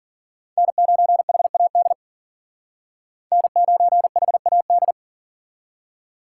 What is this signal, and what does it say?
Morse code, heard as a keyed tone, about 53 wpm, 700 Hz. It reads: N9HAD N9HAD